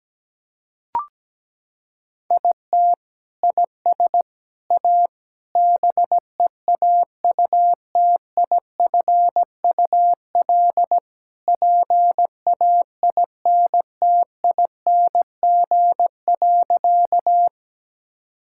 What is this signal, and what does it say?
Morse code, heard as a keyed tone, about 17 wpm, 700 Hz. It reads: IT IS A BEAUTIFUL PAINTING.
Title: Sentences from Top 1000 Words - Review Part 6 - 17wpm